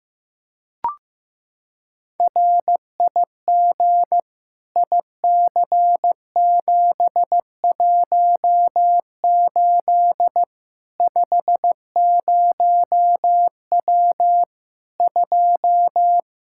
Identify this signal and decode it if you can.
Morse code, heard as a keyed tone, about 15 wpm, 700 Hz. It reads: RIG IC718 50W 2
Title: K1USN SST QSO Practice 15wpm